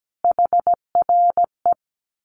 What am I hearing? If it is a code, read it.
Morse code, heard as a keyed tone, about 17 wpm, 700 Hz. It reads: HRE
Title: Common QSO Phrases 17wpm